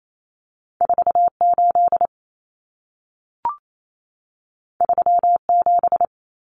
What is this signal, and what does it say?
Morse code, heard as a keyed tone, about 28 wpm, 700 Hz. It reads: 48 37
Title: Numbers - 2-Digit 28wpm